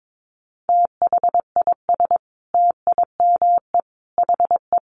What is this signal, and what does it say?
Morse code, heard as a keyed tone, about 22 wpm, 700 Hz. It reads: THIS TIME HE
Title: Sentences from Top 1200 Words 22wpm